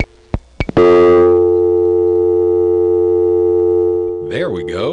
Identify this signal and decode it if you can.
Dial tone (North America), 350 + 440 Hz, with speech.
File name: phonetrips / HarrisonbgVA1HQ